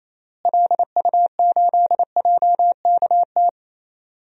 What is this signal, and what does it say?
Morse code, heard as a keyed tone, about 28 wpm, 700 Hz. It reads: LU8JKT
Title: DXCC-100 Call Signs - 6 Characters - Repeated 3x - 28wpm